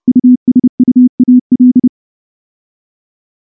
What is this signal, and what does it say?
Morse code, heard as a keyed tone, about 28 wpm, 250 Hz. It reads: USUAL
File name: morse / morse-training-30WPM